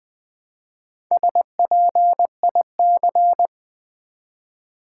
Morse code, heard as a keyed tone, about 20 wpm, 700 Hz. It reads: SPIC
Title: Build A Word - 1 Letter At A Time - 3 to 5-Letter Words - 20wpm